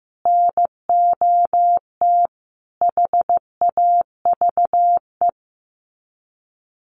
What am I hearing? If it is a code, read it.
Morse code, heard as a keyed tone, about 15 wpm, 700 Hz. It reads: NOT HAVE
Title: Sets of 4 Words 15wpm